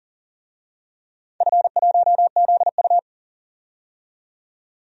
Morse code, heard as a keyed tone, about 38 wpm, 700 Hz. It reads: F1ZU
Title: DXCC-100 Call Signs - 4 Characters - 40wpm